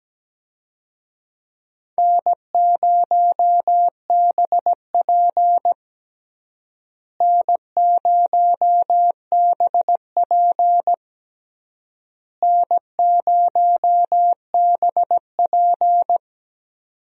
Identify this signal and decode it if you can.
Morse code, heard as a keyed tone, about 17 wpm, 700 Hz. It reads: N0BP N0BP N0BP
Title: USA Call Signs - 4 Characters - Repeated 3x - 17wpm